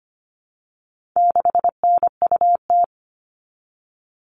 Morse code, heard as a keyed tone, about 25 wpm, 700 Hz. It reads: 6NUT
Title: All Cut Numbers - 4-Digit 25wpm